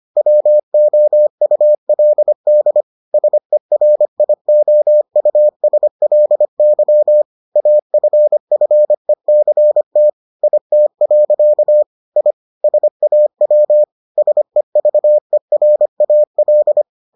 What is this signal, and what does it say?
Morse code, heard as a keyed tone, about 25 wpm, 600 Hz. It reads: WOULD SERIOUSLY AFFECT IT. I SAW SEVERAL